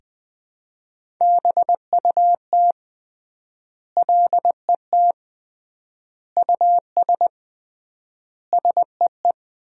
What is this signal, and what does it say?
Morse code, heard as a keyed tone, about 20 wpm, 700 Hz. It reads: BUT LET US SEE
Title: 3x Word-Spacing - Sentences from Top 500 Words - 20wpm